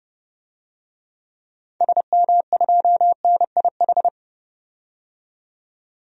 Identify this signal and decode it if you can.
Morse code, heard as a keyed tone, about 30 wpm, 700 Hz. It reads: SM2NIH
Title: DXCC-100 Call Signs - 6 Characters - 30wpm